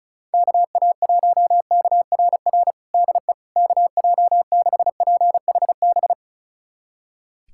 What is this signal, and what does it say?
Morse code, heard as a keyed tone, about 34 wpm, 700 Hz. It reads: KA1KRR DE KJ6PHB
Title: USA Callsign DE Callsign - 6 Characters - 35wpm